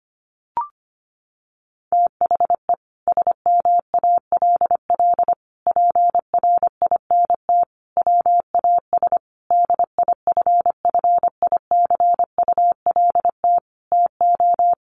Morse code, heard as a keyed tone, about 25 wpm, 700 Hz. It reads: THE SMALL PRINT WAS DIFFICULT TO
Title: Sentences from Top 1400 Words 25wpm